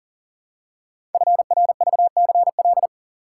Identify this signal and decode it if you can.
Morse code, heard as a keyed tone, about 40 wpm, 700 Hz. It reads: FRVCL
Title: Code Groups - 5 Letters - 40wpm